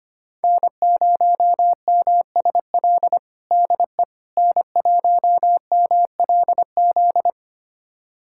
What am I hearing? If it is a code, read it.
Morse code, heard as a keyed tone, about 25 wpm, 700 Hz. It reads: N0MSL DE N1MLZ
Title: USA Callsign DE Callsign - 5 Characters - 25wpm